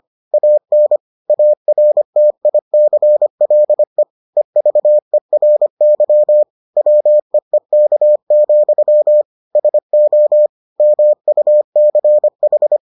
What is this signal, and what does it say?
Morse code, heard as a keyed tone, about 25 wpm, 600 Hz. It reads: AN ARTICLE EVERY WEEK, SO MUCH